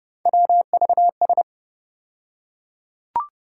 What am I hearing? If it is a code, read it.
Morse code, heard as a keyed tone, about 30 wpm, 700 Hz. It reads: WVS
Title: USA Call Sign Postfixes - 3 Letters 30wpm